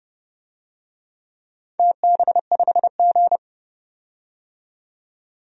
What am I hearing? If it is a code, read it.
Morse code, heard as a keyed tone, about 32 wpm, 700 Hz. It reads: TB5G
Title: DXCC-100 Call Signs - 4 Characters - 30wpm